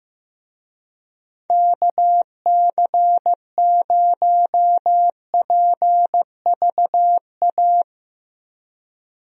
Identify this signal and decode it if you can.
Morse code, heard as a keyed tone, about 15 wpm, 700 Hz. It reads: KC0PVA
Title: USA Call Signs - 6 Characters - Repeated 2x - 15wpm